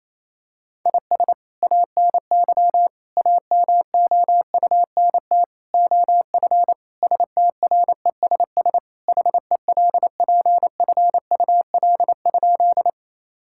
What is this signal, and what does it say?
Morse code, heard as a keyed tone, about 28 wpm, 700 Hz. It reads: IS ANY AMOUNT OF STRESS HELPFUL?
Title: Sentences from Top 1000 Words - Review Part 6 - 28wpm